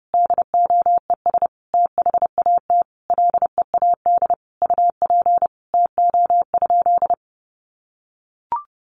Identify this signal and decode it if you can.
Morse code, heard as a keyed tone, about 30 wpm, 700 Hz. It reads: DOES THAT LEAD UP TO?
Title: Sentences from Top 500 Words 30wpm